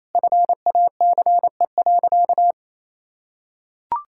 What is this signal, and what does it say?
Morse code, heard as a keyed tone, about 28 wpm, 700 Hz. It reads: FACE.
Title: Sentences from Top 1600 Words 28wpm